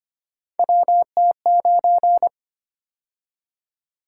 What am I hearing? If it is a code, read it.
Morse code, heard as a keyed tone, about 25 wpm, 700 Hz. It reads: WT9
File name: Three Letter-Number Combinations - 25wpm